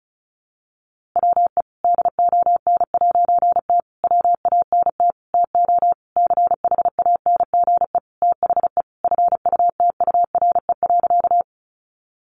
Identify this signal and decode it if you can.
Morse code, heard as a keyed tone, about 34 wpm, 700 Hz. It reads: WE DON'T WANT TO CHANGE THE FUTURE.
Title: A Sound Of Thunder 35wpm